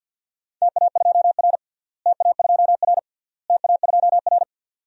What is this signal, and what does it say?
Morse code, heard as a keyed tone, about 49 wpm, 700 Hz. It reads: TA2R TA2R TA2R
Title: DXCC-100 Call Signs - 4 Characters - Repeated 3x - 50wpm